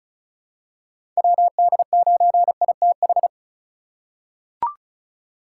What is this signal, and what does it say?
Morse code, heard as a keyed tone, about 34 wpm, 700 Hz. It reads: WD9ITH
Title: USA Call Signs - 6 Characters - 3x Character Spacing - 35wpm